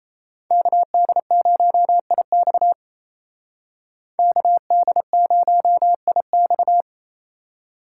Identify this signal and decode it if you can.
Morse code, heard as a keyed tone, about 33 wpm, 700 Hz. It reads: KD0IX KD0IX
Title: Speed Racing - USA 2x2 Call Signs 28wpm